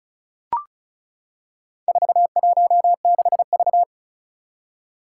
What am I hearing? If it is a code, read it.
Morse code, heard as a keyed tone, about 36 wpm, 690 Hz. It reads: 416V
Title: Four Challenging Letter-Number Combinations 35wpm